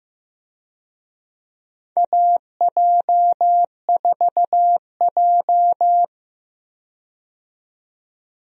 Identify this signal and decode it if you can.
Morse code, heard as a keyed tone, about 15 wpm, 700 Hz. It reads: AJ4J
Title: USA Call Signs - 4 Characters - 15wpm